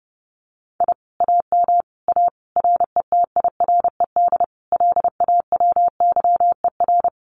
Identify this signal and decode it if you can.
Morse code, heard as a keyed tone, about 30 wpm, 700 Hz. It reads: I AM A RETIRED LAWYER
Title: Common QSO Phrases Encore 30wpm